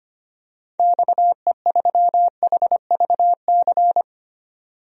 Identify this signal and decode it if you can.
Morse code, heard as a keyed tone, about 25 wpm, 700 Hz. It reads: XE3HVC
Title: DXCC-100 Call Signs - 6 Characters - 25wpm